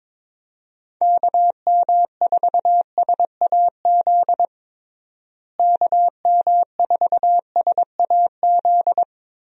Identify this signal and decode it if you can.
Morse code, heard as a keyed tone, about 21 wpm, 700 Hz. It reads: KM4SAZ KM4SAZ
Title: USA Call Signs - 6 Characters - Repeated 2x - 22wpm